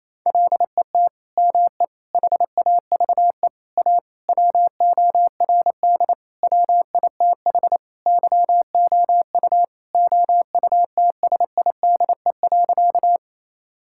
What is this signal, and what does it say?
Morse code, heard as a keyed tone, about 28 wpm, 700 Hz. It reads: LET ME HAVE A WORD WITH YOU OUTSIDE.
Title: Sentences from Top 1000 Words - Review Part 6 - 28wpm